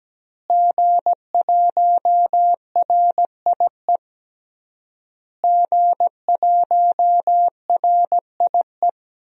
Morse code, heard as a keyed tone, about 17 wpm, 700 Hz. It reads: G1RIE G1RIE
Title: DXCC-100 Call Signs - 5 Characters - Repeated 3x - 17wpm